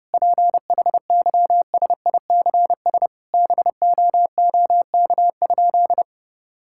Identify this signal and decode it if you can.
Morse code, heard as a keyed tone, about 30 wpm, 700 Hz. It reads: PHYSICS BOOK?
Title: Sentences from Top 1000 Words - Review Part 5 - 30wpm